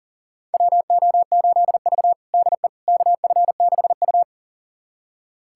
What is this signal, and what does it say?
Morse code, heard as a keyed tone, about 40 wpm, 700 Hz. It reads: WO8V DE KF6U
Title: USA Callsign DE Callsign - 4 Characters - 40wpm